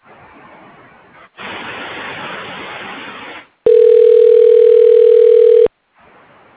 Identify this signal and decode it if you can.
Ringback tone (North America), 440 + 480 Hz.